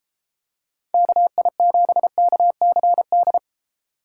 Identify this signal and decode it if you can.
Morse code, heard as a keyed tone, about 33 wpm, 700 Hz. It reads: KI7KCD